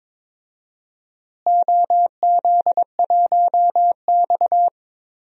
Morse code, heard as a keyed tone, about 22 wpm, 700 Hz. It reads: OZ1X
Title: DXCC-100 Call Signs - 4 Characters - Repeated 3x - 22wpm